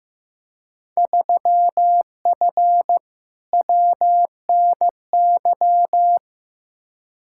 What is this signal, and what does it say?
Morse code, heard as a keyed tone, about 15 wpm, 700 Hz. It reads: 3F WNY
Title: Field Day Exchanges - 15wpm